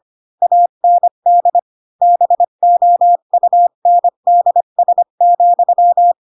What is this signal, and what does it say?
Morse code, heard as a keyed tone, about 25 wpm, 700 Hz. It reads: AND BOUNDS,
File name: Pinocchio / Pinocchio ch3 20 25 ik1hge